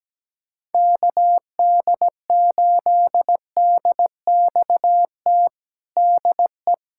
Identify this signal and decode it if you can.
Morse code, heard as a keyed tone, about 17 wpm, 700 Hz. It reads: KD8DXT DE
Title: USA Callsign DE Callsign - 6 Characters - 17wpm